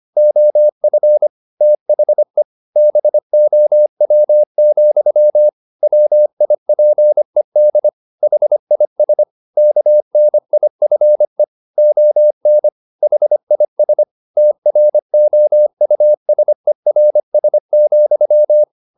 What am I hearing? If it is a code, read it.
Morse code, heard as a keyed tone, about 25 wpm, 600 Hz. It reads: OF THE BOW, WIPED HIS KNIFE ON HIS TROUSERS,
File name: the old man and the sea part 1 25wpm